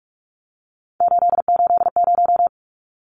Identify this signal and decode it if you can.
Morse code, heard as a keyed tone, about 44 wpm, 700 Hz. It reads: 880
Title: Numbers - 3-Digit 45wpm